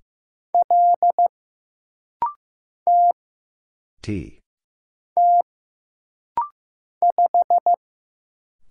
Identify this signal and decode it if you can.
Morse code, heard as a keyed tone, about 15 wpm, 700 Hz. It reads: L T T 5
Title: Single Letter-Number Alt - Rapid-Fire 15wpm